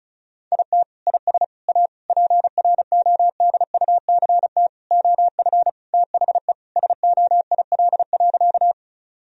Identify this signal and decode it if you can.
Morse code, heard as a keyed tone, about 34 wpm, 700 Hz. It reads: IT IS A PRODUCT OF THE SOIL.
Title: Sentences from Top 1000 Words - Review Part 4 - 35wpm